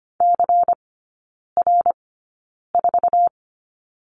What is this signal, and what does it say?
Morse code, heard as a keyed tone, about 25 wpm, spaced as for 9 wpm, 700 Hz. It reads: CR4
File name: course / 25:10wpm / 093 Lesson 16e - Letter c - Three Characters 25wpm